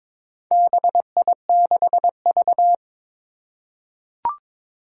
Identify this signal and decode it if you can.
Morse code, heard as a keyed tone, about 22 wpm, 700 Hz. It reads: BI6V
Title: Four Challenging Letter-Number Combinations 22wpm